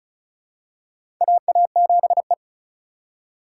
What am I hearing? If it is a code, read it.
Morse code, heard as a keyed tone, about 34 wpm, 700 Hz. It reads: AA7E
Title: USA Call Signs - 4 Characters - 3x Character Spacing - 35wpm